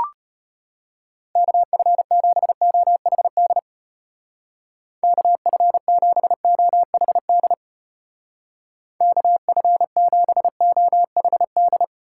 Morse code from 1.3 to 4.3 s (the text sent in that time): KF7OHD